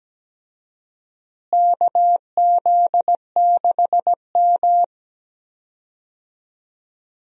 Morse code, heard as a keyed tone, about 17 wpm, 700 Hz. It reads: KZ6M
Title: USA Call Signs - 4 Characters - Repeated 3x - 17wpm